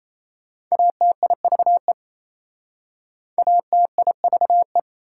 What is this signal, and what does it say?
Morse code, heard as a keyed tone, about 37 wpm, 700 Hz. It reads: ATIVE ATIVE